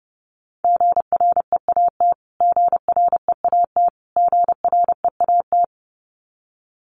Morse code, heard as a keyed tone, about 30 wpm, 700 Hz. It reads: GREAT GREAT GREAT
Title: Top 500 Words Repeated 30wpm